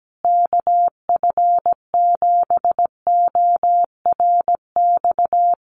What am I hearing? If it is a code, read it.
Morse code, heard as a keyed tone, about 17 wpm, 700 Hz. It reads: KF7ORX
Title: USA Call Signs Repeated 3x 17wpm